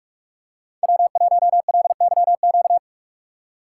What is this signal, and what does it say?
Morse code, heard as a keyed tone, about 43 wpm, 700 Hz. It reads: W1LYQ
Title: USA Call Signs - 5 Characters - 8x Character Spacing - 45wpm